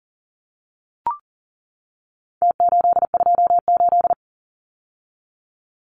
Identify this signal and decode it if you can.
Morse code, heard as a keyed tone, about 40 wpm, 700 Hz. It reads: T828
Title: Numbers - Contest Style - 4-Digit 40wpm